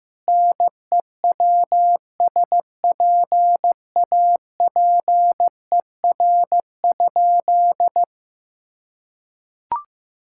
Morse code, heard as a keyed tone, about 15 wpm, 700 Hz. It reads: NEWSPAPER?
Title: Sentences from Top 1000 Words - Review Part 6 - 15wpm